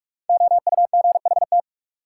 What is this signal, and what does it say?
Morse code, heard as a keyed tone, about 46 wpm, 700 Hz. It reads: OUGHT
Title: Sets of 5 Words Encore - New Word Drill - 45wpm